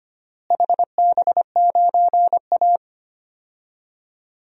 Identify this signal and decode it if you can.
Morse code, heard as a keyed tone, about 25 wpm, 700 Hz. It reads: HB9A
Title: DXCC-100 Call Signs - 4 Characters - Repeated 2x - 25wpm